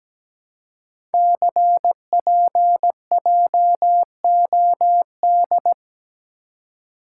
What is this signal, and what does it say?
Morse code, heard as a keyed tone, about 17 wpm, 700 Hz. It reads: CPJOD
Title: Code Groups - 5 Letters - 17wpm